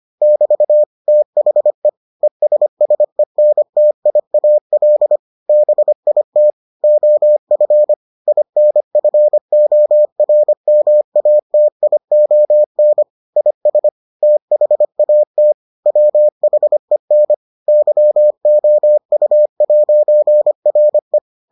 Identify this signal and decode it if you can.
Morse code, heard as a keyed tone, about 25 wpm, 600 Hz. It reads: <BT> THE ESSENTIAL BIT OF INFORMATION IS THAT WHEN YOU'RE